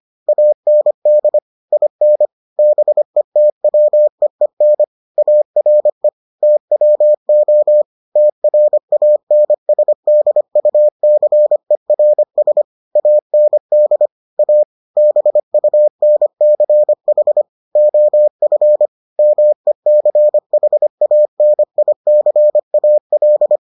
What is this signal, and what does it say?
Morse code, heard as a keyed tone, about 25 wpm, 600 Hz. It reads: AND IN BETWEEN ARE TWO TRANSDUCERS AND A BUNCH OF MECHANICAL